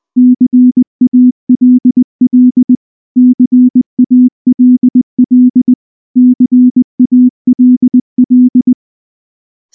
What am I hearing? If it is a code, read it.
Morse code, heard as a keyed tone, about 20 wpm, 250 Hz. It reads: CALL CALL CALL
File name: morse / morse-training-20WPM-5